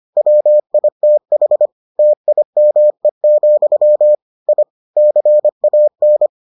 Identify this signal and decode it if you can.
Morse code, heard as a keyed tone, about 25 wpm, 600 Hz. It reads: WITH TIME, I CAN